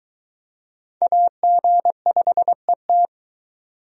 Morse code, heard as a keyed tone, about 23 wpm, 700 Hz. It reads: AG5ET